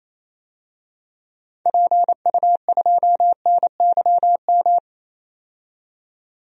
Morse code, heard as a keyed tone, about 28 wpm, 700 Hz. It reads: PU2NYM
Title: DXCC-100 Call Signs - 6 Characters - 28wpm